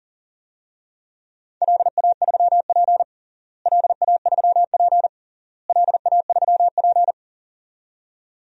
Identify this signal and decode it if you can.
Morse code, heard as a keyed tone, about 40 wpm, 700 Hz. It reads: LA3P LA3P LA3P
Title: DXCC-100 Call Signs - 4 Characters - Repeated 3x - 40wpm